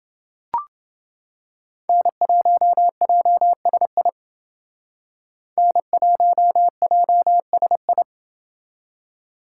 Morse code, heard as a keyed tone, about 28 wpm, 700 Hz. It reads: N1JSI N1JSI